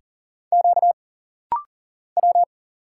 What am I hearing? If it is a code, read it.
Morse code, heard as a keyed tone, about 39 wpm, 700 Hz. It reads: Q W
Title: Single Letters - Rapid-Fire 40wpm